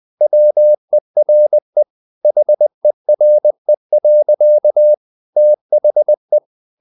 Morse code, heard as a keyed tone, about 20 wpm, 600 Hz. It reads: WERE HERE. THE
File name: the old man and the sea part 2 20wpm